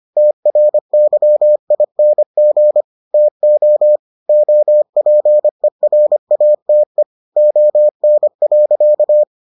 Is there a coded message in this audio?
Yes: Morse code from 0.2 to 9.2 s, heard as a keyed tone, about 25 wpm, 600 Hz. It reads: TRYING TO OPERATE ON.